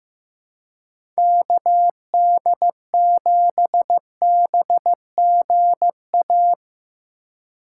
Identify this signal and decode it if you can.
Morse code, heard as a keyed tone, about 15 wpm, 700 Hz. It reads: KD7BGA
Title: USA Call Signs - 6 Characters - 8x Character Spacing - 15wpm